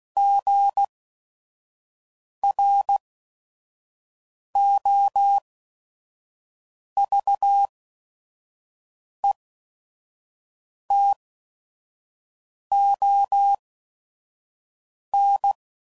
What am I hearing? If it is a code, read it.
Morse code, heard as a keyed tone, about 16 wpm, spaced as for 5 wpm, 800 Hz. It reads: GROVETON
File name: CW Code Groups @5WPM / fromSM's / Cities